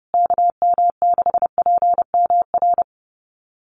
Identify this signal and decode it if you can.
Morse code, heard as a keyed tone, about 30 wpm, 700 Hz. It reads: KM6PMR
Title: USA Call Signs Repeated 2x 30wpm